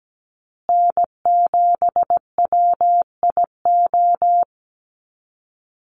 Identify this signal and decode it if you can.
Morse code, heard as a keyed tone, about 17 wpm, 700 Hz. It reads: N7WIO
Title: USA Call Signs Repeated 2x 17wpm